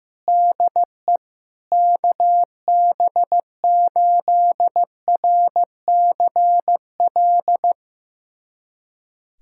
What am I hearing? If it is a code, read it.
Morse code, heard as a keyed tone, about 15 wpm, 700 Hz. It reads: DE KB8RCL
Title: USA Callsign DE Callsign 15wpm